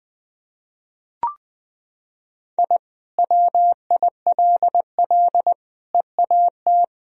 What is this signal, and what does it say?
Morse code, heard as a keyed tone, about 20 wpm, 700 Hz. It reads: I WILL EAT
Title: Sentences from Top 1000 Words - Review Part 1 - 20wpm